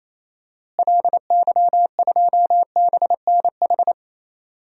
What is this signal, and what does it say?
Morse code, heard as a keyed tone, about 28 wpm, 700 Hz. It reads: LY2BNH
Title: DXCC-100 Call Signs - 6 Characters - Repeated 3x - 28wpm